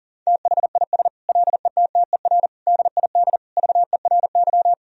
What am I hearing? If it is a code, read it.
Morse code, heard as a keyed tone, about 37 wpm, 700 Hz. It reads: THIS LETTER DID VERY
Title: Sentences from Top 1000 Words - Review Part 4 - 40wpm